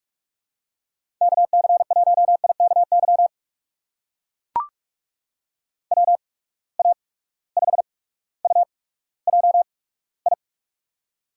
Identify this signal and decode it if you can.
Morse code, heard as a keyed tone, about 42 wpm, spaced as for 19 wpm, 700 Hz. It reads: KC1IKY WA5UJI
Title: USA Call Signs - 6 Characters - 8x Character Spacing - 45wpm